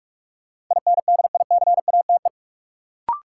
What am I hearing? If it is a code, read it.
Morse code, heard as a keyed tone, about 46 wpm, 700 Hz. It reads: INDICATE
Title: Sentences from Top 1300 Words - New Word Drill - 45wpm